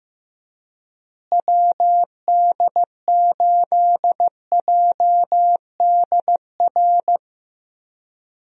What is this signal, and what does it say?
Morse code, heard as a keyed tone, about 15 wpm, 700 Hz. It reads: WD8JDR
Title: USA Call Signs - 6 Characters - Repeated 3x - 15wpm